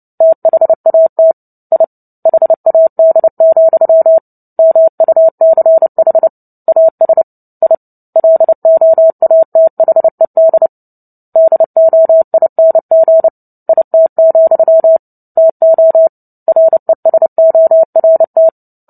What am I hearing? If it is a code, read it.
Morse code, heard as a keyed tone, about 30 wpm, 650 Hz. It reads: THAT I HAD, MUCH AS I LOATHED DOING IT, TO RESORT
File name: Book Two / 03-Chapter 3 - 29 WPM